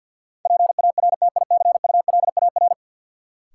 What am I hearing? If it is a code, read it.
Morse code, heard as a keyed tone, about 50 wpm, 700 Hz. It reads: PARTICULAR
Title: Sentences from Top 600 Words - New Word Drill - 50wpm